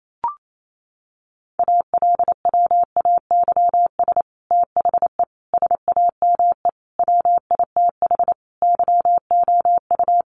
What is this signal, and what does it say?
Morse code, heard as a keyed tone, about 28 wpm, 700 Hz. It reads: ALWAYS THE SAME WITH YOU
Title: Sentences from Top 300 Words 28wpm